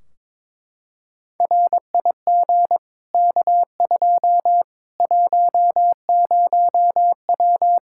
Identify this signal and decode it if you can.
Morse code, heard as a keyed tone, about 21 wpm, 700 Hz. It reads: RIG K2 10W
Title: Practice QSO - Exchange 2 - Info - 22wpm